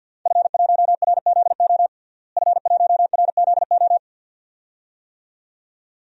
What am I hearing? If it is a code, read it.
Morse code, heard as a keyed tone, about 51 wpm, 700 Hz. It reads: F1RZO F1RZO
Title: DXCC-100 Call Signs - 5 Characters - Repeated 2x - 50wpm